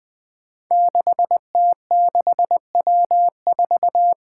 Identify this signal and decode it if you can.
Morse code, heard as a keyed tone, about 20 wpm, 700 Hz. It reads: 6T6W4
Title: Five Challenging Letter-Number Combinations 20wpm